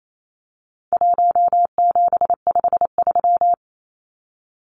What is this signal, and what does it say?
Morse code, heard as a keyed tone, about 28 wpm, 700 Hz. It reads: 1753
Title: Numbers - 4-Digit 28wpm